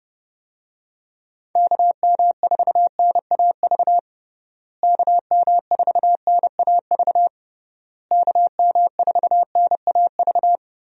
Morse code, heard as a keyed tone, about 30 wpm, 700 Hz. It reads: KM4NAV KM4NAV KM4NAV
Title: USA Call Signs - 6 Characters - Repeated 3x - 30wpm